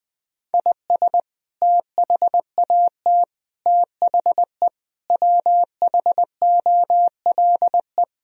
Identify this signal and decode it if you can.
Morse code, heard as a keyed tone, about 20 wpm, 700 Hz. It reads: IS THAT THE WHOLE
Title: Sentences from Top 1000 Words - Review Part 4 - 20wpm